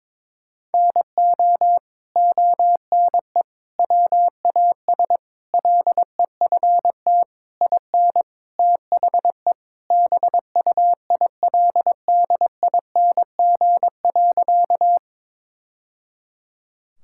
Morse code, heard as a keyed tone, about 22 wpm, 700 Hz. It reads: NO ONE WAS LEFT IN THE BUILDING.
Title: Sentences from Top 1000 Words - Review Part 5 - 22wpm